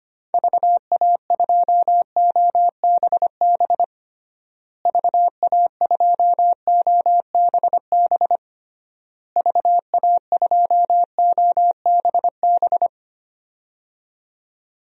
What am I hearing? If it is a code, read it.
Morse code, heard as a keyed tone, about 25 wpm, 700 Hz. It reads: VA2OBB VA2OBB VA2OBB
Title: DXCC-100 Call Signs - 6 Characters - Repeated 3x - 25wpm